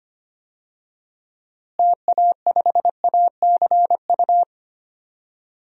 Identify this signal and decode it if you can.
Morse code, heard as a keyed tone, about 25 wpm, 700 Hz. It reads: TA5ACU
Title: DXCC-100 Call Signs - 6 Characters - Repeated 2x - 25wpm